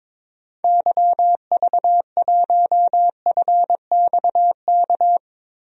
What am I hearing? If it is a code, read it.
Morse code, heard as a keyed tone, about 22 wpm, 700 Hz. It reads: YV1FXK
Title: DXCC-100 Call Signs - 6 Characters - Repeated 3x - 22wpm